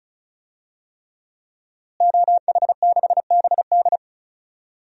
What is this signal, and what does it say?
Morse code, heard as a keyed tone, about 34 wpm, 700 Hz. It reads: OH6BD
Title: DXCC-100 Call Signs - 5 Characters - Repeated 3x - 35wpm